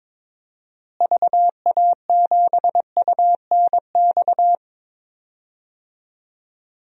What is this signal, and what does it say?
Morse code, heard as a keyed tone, about 22 wpm, 700 Hz. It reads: VA7UNX
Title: DXCC-100 Call Signs - 6 Characters - Repeated 2x - 22wpm